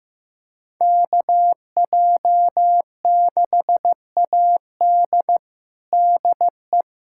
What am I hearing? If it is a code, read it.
Morse code, heard as a keyed tone, about 15 wpm, 700 Hz. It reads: KJ6AD DE
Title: USA Callsign DE Callsign - 5 Characters - 15wpm